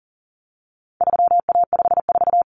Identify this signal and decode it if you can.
Morse code, heard as a keyed tone, about 40 wpm, 700 Hz. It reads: 3A54
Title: Common Cut Numbers - 4-Digit 40wpm